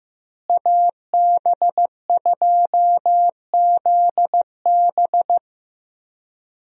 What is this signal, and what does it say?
Morse code, heard as a keyed tone, about 15 wpm, 700 Hz. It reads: AB2ZB
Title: USA Call Signs - 5 Characters - Repeated 3x - 15wpm